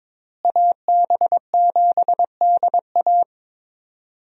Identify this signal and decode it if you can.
Morse code, heard as a keyed tone, about 22 wpm, 700 Hz. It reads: AB7DA
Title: USA Call Signs - 5 Characters - Repeated 3x - 22wpm